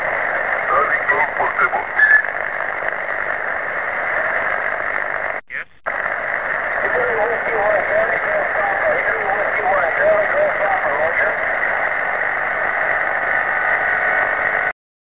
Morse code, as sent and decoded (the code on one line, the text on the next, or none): none